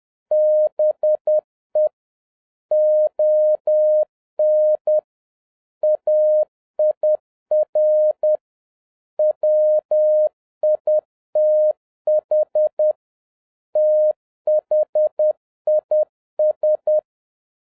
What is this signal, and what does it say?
Morse code, heard as a keyed tone, about 10 wpm, 610 Hz. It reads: BE ON AIR WITH THIS
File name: 2012-01-mp3 / n09-ITO JR7OEF-10wpm